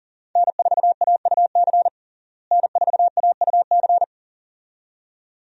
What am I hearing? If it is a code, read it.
Morse code, heard as a keyed tone, about 40 wpm, 700 Hz. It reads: N4AUC N4AUC
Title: USA Call Signs - 5 Characters - Repeated 2x - 40wpm